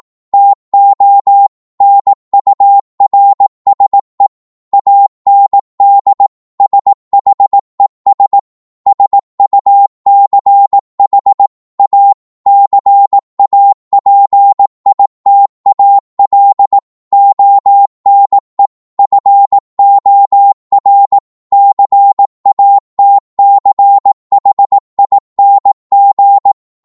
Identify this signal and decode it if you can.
Morse code, heard as a keyed tone, about 18 wpm, 800 Hz. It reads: TO NURSE AND SHES SUCH A CAPITAL ONE FOR CATCHING
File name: Morse / psg-18wpm